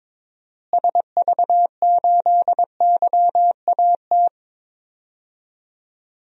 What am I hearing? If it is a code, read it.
Morse code, heard as a keyed tone, about 22 wpm, 700 Hz. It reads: SV8YAT
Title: DXCC-100 Call Signs - 6 Characters - Repeated 2x - 22wpm